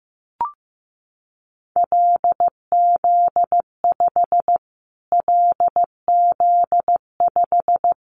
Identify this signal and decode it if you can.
Morse code, heard as a keyed tone, about 15 wpm, 700 Hz. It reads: LZ5 LZ5
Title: Callsign Prefixes for Top 25 DXCC - Repeated 3x - 15wpm